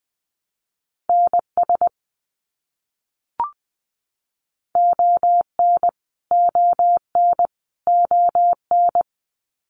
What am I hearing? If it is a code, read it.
Morse code, heard as a keyed tone, about 20 wpm, 700 Hz. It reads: NS ON ON ON
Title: Canadian Provinces and Territories - Repeated 3x - 20wpm